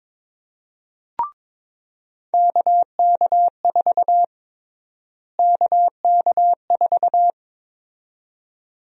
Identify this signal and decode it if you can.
Morse code, heard as a keyed tone, about 21 wpm, 700 Hz. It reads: KK4 KK4
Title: USA Call Sign Prefixes - Repeated 2x - 22wpm